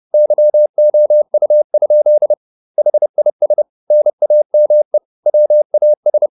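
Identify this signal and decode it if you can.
Morse code, heard as a keyed tone, about 30 wpm, 600 Hz. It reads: YOU? HIS NAME WAS
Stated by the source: Morse code at 30 wpm, the novel Never by Ken Follett in English, from CW med Gnister.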